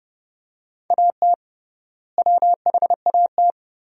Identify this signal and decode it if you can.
Morse code, heard as a keyed tone, about 32 wpm, 700 Hz. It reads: AT WHAT
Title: 3x Word-Spacing - Sentences from Top 100 Words - 30wpm